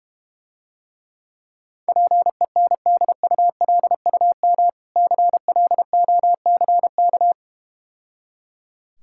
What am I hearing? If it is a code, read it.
Morse code, heard as a keyed tone, about 32 wpm, 700 Hz. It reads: PENDULUM CLOCK